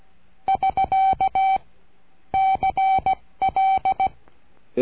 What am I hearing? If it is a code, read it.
Morse code, heard as a keyed tone, about 16 wpm, 770 Hz. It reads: <SK> CL
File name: cursos / CURSO DE TELEGRAFIA / Telegrafia - Trilha 22